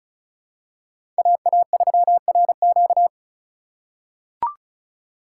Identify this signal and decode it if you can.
Morse code, heard as a keyed tone, about 36 wpm, 700 Hz. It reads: AA3RQ
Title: USA Call Signs - 5 Characters - 4x Character Spacing - 35wpm